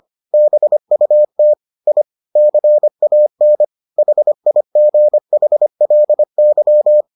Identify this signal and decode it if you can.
Morse code, heard as a keyed tone, about 25 wpm, 600 Hz. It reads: BUT I CAN HIGHLY